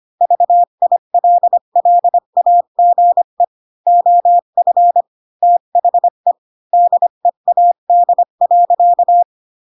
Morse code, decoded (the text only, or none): VILLAGE OF THE DEAD.